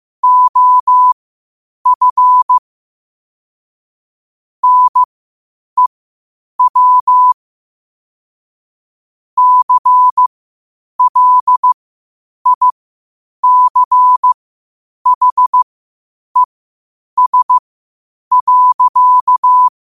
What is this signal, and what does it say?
Morse code, heard as a keyed tone, about 15 wpm, spaced as for 8 wpm, 1000 Hz. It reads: OF NEW CLICHES.
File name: qotdfiles / 8wpm